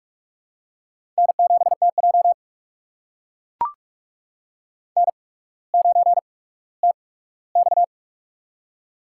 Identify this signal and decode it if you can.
Morse code, heard as a keyed tone, about 44 wpm, spaced as for 19 wpm, 700 Hz. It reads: N7TJ N9TX
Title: USA Call Signs - 4 Characters - 8x Character Spacing - 45wpm